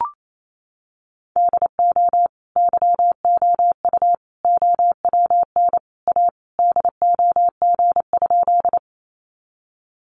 Morse code, heard as a keyed tone, about 28 wpm, 700 Hz. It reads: DO YOU OWN A DOG?
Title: Sentences from Top 400 Words 28wpm